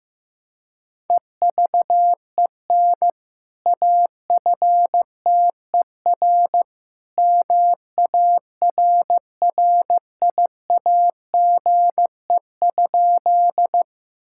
Morse code, heard as a keyed tone, about 16 wpm, 700 Hz. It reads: EVEN AFTER MARRIAGE?
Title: Sentences from Top 1000 Words - Review Part 5 - 15wpm